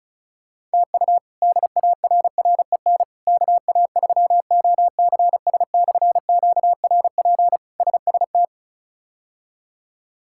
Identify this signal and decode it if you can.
Morse code, heard as a keyed tone, about 34 wpm, 700 Hz. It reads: TU DARREN KA3OCS/QRP SST
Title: K1USN SST QSO Practice - 35wpm